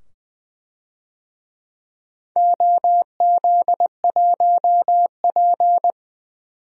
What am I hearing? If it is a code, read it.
Morse code, heard as a keyed tone, about 20 wpm, 700 Hz. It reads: OZ1P
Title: DXCC-100 Call Signs - 4 Characters - 20wpm